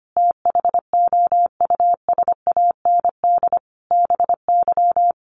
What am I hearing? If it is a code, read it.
Morse code, heard as a keyed tone, about 25 wpm, 700 Hz. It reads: THOUSAND BY
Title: Sentences from Top 1100 Words 25wpm